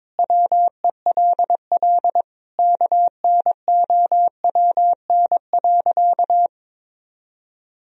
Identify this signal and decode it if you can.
Morse code, heard as a keyed tone, about 22 wpm, 700 Hz. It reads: WELL KNOWN.
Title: Sentences from Top 1000 Words - Review Part 3 - 22wpm